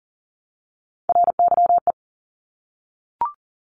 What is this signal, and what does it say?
Morse code, heard as a keyed tone, about 38 wpm, 700 Hz. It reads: RYE
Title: Multi-letter Phonemes Repeated 3x 40wpm